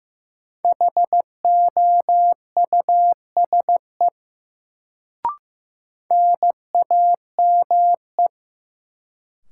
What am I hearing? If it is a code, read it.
Morse code, heard as a keyed tone, about 15 wpm, 700 Hz. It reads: HOUSE NAME
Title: Sentences from Top 200 Words - New Word Drill - 15wpm